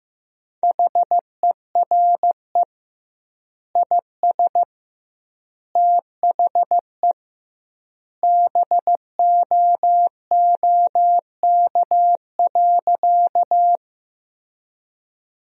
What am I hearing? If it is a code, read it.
Morse code, heard as a keyed tone, about 15 wpm, 700 Hz. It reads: HERE IS THE BOOK.
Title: 2x Word-Spacing - Sentences from Top 300 Words - 15wpm